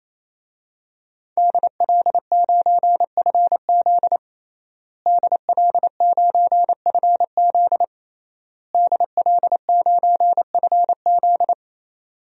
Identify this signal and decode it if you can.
Morse code, heard as a keyed tone, about 28 wpm, 700 Hz. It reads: DL9FZ DL9FZ DL9FZ
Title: DXCC-100 Call Signs - 5 Characters - Repeated 3x - 28wpm